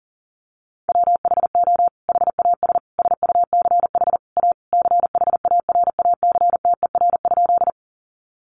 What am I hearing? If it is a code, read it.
Morse code, heard as a keyed tone, about 40 wpm, 700 Hz. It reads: WHO HAS SUCH A CHARACTER?
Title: Sentences from Top 1200 Words 40wpm